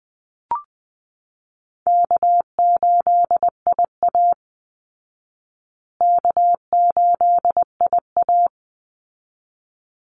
Morse code, heard as a keyed tone, about 20 wpm, 700 Hz. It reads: K8IA K8IA
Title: Call Signs Repeated 3x - 20wpm